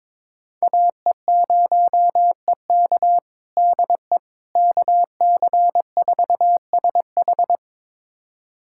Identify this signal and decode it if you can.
Morse code, heard as a keyed tone, about 22 wpm, 700 Hz. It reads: AE0EK DE KC4SH
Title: USA Callsign DE Callsign - 5 Characters - 22wpm